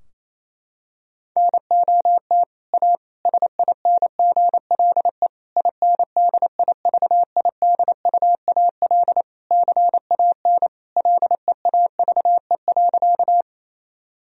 Morse code, heard as a keyed tone, about 28 wpm, 700 Hz. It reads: NOT A SINGLE INDIVIDUAL CAN LEAVE.
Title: Sentences from Top 1000 Words - Review Part 4 - 28wpm